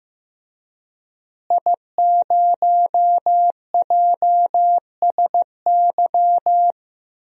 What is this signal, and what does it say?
Morse code, heard as a keyed tone, about 15 wpm, 700 Hz. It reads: I0JSY
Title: DXCC-100 Call Signs - 5 Characters - Repeated 3x - 15wpm